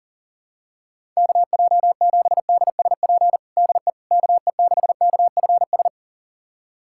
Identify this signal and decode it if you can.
Morse code, heard as a keyed tone, about 40 wpm, 700 Hz. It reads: KJ7DSP DE KE6KFS
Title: USA Callsign DE Callsign - 6 Characters - 40wpm